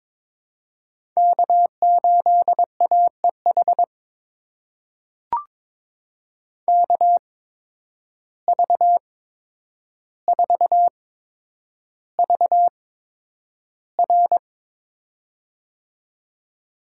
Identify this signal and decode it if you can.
Morse code, heard as a keyed tone, about 22 wpm, spaced as for 9 wpm, 700 Hz. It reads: K8AEH KV4VR
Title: USA Call Signs - 8x Character Spacing - 22wpm